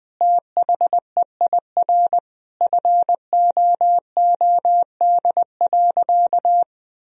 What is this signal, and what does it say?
Morse code, heard as a keyed tone, about 20 wpm, 700 Hz. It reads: THEIR FOOD.
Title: Sentences from Top 1000 Words - Review Part 1 - 20wpm